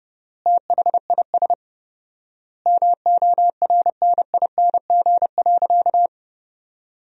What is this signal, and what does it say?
Morse code, heard as a keyed tone, about 32 wpm, 700 Hz. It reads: THIS MORNING.
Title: 4x Word-Spacing - Sentences from Top 400 Words - 30wpm